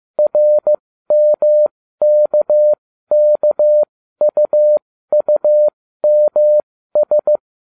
Morse code, heard as a keyed tone, about 15 wpm, spaced as for 13 wpm, 600 Hz. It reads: RMKKUUMS